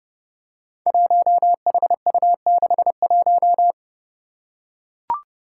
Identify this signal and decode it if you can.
Morse code, heard as a keyed tone, about 29 wpm, 700 Hz. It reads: 1HU61
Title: Five Challenging Letter-Number Combinations 30wpm